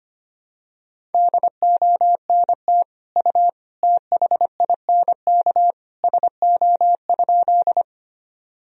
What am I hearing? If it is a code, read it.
Morse code, heard as a keyed tone, about 25 wpm, 700 Hz. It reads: DONT U THINK SO?
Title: Sentences from Top 300 Words and Abbreviations 25wpm